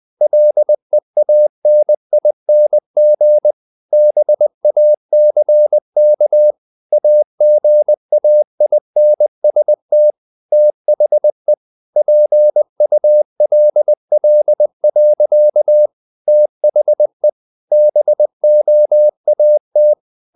Morse code, heard as a keyed tone, about 20 wpm, 600 Hz. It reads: LEANING BACK AGAINST THE PULL. THE BOAT
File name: the old man and the sea / the old man and the sea part 1 20wpm